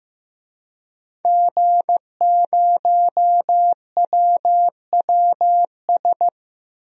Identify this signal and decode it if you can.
Morse code, heard as a keyed tone, about 15 wpm, 700 Hz. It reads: G0WWS
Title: DXCC-100 Call Signs - 5 Characters - Repeated 2x - 15wpm